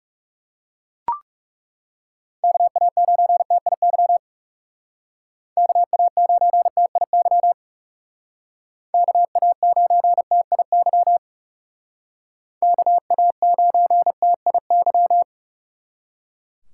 Morse code, heard as a keyed tone, about 37 wpm, 700 Hz. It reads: KA9TIY KA9TIY KA9TIY KA9TIY
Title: Speed Racing - USA 2x3 Call Signs 30wpm